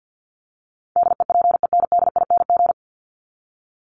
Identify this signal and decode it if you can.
Morse code, heard as a keyed tone, about 50 wpm, 700 Hz. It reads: DEPENDING
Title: ING Suffix - Top 100 Words with Base Words 50wpm